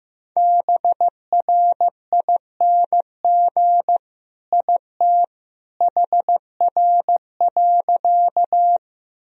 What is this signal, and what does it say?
Morse code, heard as a keyed tone, about 15 wpm, 700 Hz. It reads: BRING IT HR.
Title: Sentences from Top 500 Words and Abbreviations 15wpm